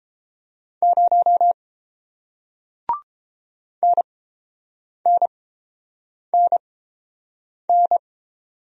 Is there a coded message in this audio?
Yes: Morse code from 0.8 to 8.0 s, heard as a keyed tone, about 30 wpm, spaced as for 7 wpm, 700 Hz. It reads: 0 NNNN